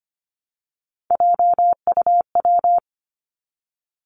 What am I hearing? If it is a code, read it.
Morse code, heard as a keyed tone, about 25 wpm, 700 Hz. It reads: JUW
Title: Three Challenging Letter-Number Combinations 25wpm